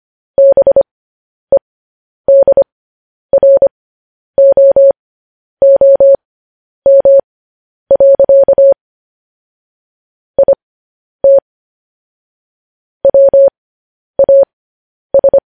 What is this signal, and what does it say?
Morse code, heard as a keyed tone, about 25 wpm, spaced as for 10 wpm, 550 Hz. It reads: BEDROOM. IT WAS
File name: podcast / News headlines-1x-25x10